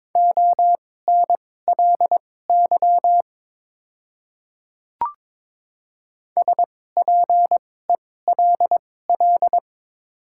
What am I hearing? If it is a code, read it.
Morse code, heard as a keyed tone, about 22 wpm, spaced as for 16 wpm, 700 Hz. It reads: ONLY SPELL
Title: 2x Character Spacing - Top 300 Words - 22wpm